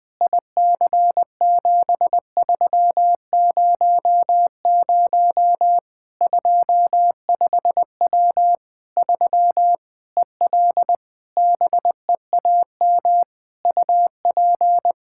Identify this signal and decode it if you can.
Morse code, heard as a keyed tone, about 20 wpm, 700 Hz. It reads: IC7300 25W 3 EL BEAM UP
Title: Practice QSO - Exchange 2 - Info - 20wpm